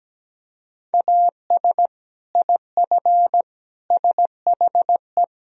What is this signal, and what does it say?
Morse code, heard as a keyed tone, about 17 wpm, 700 Hz. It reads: AS IF SHE
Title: Sentences from Top 1000 Words - Review Part 1 - 17wpm